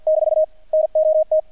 Morse code, heard as a keyed tone, about 48 wpm, 630 Hz. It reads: - NOT
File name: archived / CRSnet